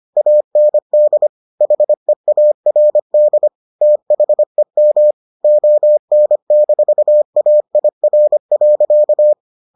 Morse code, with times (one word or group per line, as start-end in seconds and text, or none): 0.2-1.3 AND
1.6-3.5 HEARD
3.8-5.1 THEM
5.4-9.3 ON-AIR.